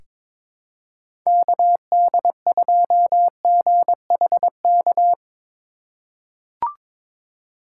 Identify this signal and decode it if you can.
Morse code, heard as a keyed tone, about 22 wpm, 700 Hz. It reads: KD2GHK
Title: USA Call Signs - 6 Characters - 3x Character Spacing - 22wpm